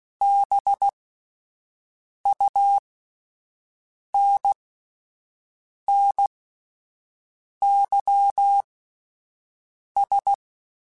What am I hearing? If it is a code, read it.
Morse code, heard as a keyed tone, about 16 wpm, spaced as for 5 wpm, 800 Hz. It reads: BUNNYS